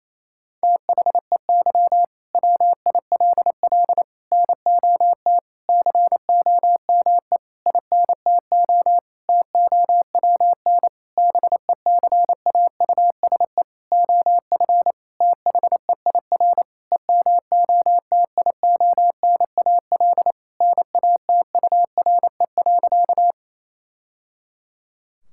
Morse code, heard as a keyed tone, about 28 wpm, 700 Hz. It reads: THEY WILL NOT COME INTO TOWN BECAUSE OF THEIR EMOTIONAL NATURE.
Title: Sentences from Top 1000 Words - Review Part 5 - 28wpm